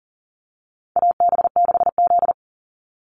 Morse code, heard as a keyed tone, about 40 wpm, 700 Hz. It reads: AB6Z
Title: Call Signs Repeated 2x - 40wpm